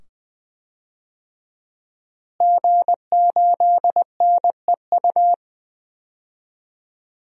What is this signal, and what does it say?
Morse code, heard as a keyed tone, about 20 wpm, 700 Hz. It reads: G8NEU
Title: DXCC-100 Call Signs - 5 Characters - Repeated 2x - 20wpm